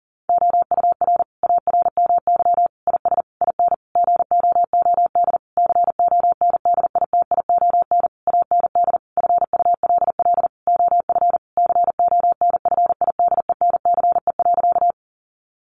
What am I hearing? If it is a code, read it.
Morse code, heard as a keyed tone, about 40 wpm, 700 Hz. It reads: OUR ARMY IS IN GOOD CONDITION AND FULL OF CONFIDENCE.
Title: Sentences from Top 1500 Words 40wpm